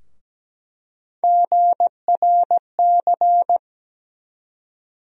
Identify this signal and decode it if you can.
Morse code, heard as a keyed tone, about 17 wpm, 700 Hz. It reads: GRC
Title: USA Call Sign Postfixes - 3 Letters 17wpm